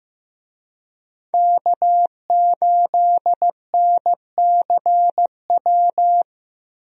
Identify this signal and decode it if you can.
Morse code, heard as a keyed tone, about 15 wpm, 700 Hz. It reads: K8NCW
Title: USA Call Signs - 5 Characters - Repeated 2x - 15wpm